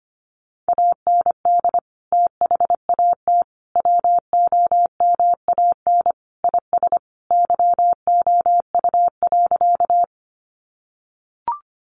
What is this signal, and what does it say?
Morse code, heard as a keyed tone, about 25 wpm, 700 Hz. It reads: AND THAT WOMAN IS YOU.
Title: Sentences from Top 1100 Words 25wpm